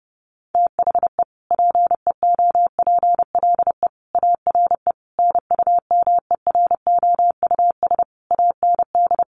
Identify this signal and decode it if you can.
Morse code, heard as a keyed tone, about 30 wpm, 700 Hz. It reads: THE PEOPLE ARE NUMEROUS AND
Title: Sentences from Top 1400 Words 30wpm